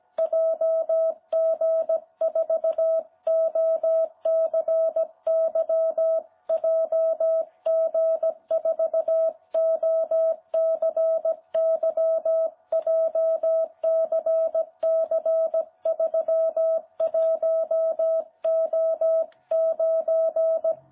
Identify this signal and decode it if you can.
Morse code, heard as a keyed tone, about 17 wpm, 660 Hz. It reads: JG4OCY JG4OCY JCC31O9
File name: jh8cbh / 2021cw